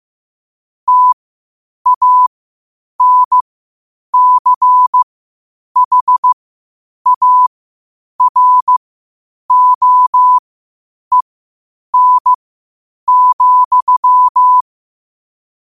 Morse code, heard as a keyed tone, about 15 wpm, spaced as for 8 wpm, 1000 Hz. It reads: TANCHAROEN,
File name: qotdfiles / 8wpm